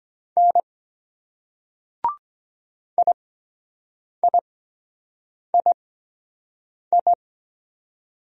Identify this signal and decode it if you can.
Morse code, heard as a keyed tone, about 23 wpm, spaced as for 7 wpm, 700 Hz. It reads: N IIII